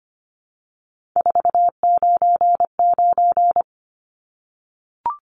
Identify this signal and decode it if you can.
Morse code, heard as a keyed tone, about 25 wpm, 700 Hz. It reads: 499
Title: Numbers - 3-Digit 25wpm